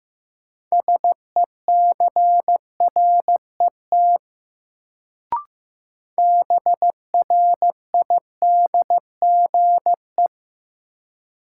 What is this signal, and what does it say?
Morse code, heard as a keyed tone, about 15 wpm, 700 Hz. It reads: SECRET BRIDGE
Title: Sentences from Top 1200 Words - New Word Drill - 15wpm